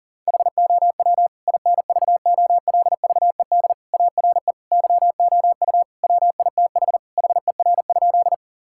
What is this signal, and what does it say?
Morse code, heard as a keyed tone, about 40 wpm, 690 Hz. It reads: HOW INVOLVED ARE YOU WITH HER?
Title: Sentences from Top 1000 Words - Review Part 5 - 40wpm